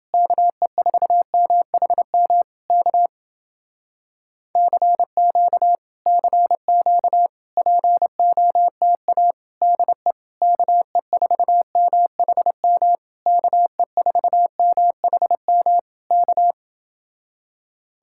Morse code, heard as a keyed tone, about 29 wpm, 700 Hz. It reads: KE4MHM K CQ CQ POTA DE KE4MHM KE4MHM K